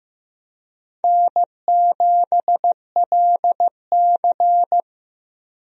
Morse code, heard as a keyed tone, about 15 wpm, 700 Hz. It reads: N7LC
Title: USA Call Signs - 4 Characters - Repeated 2x - 15wpm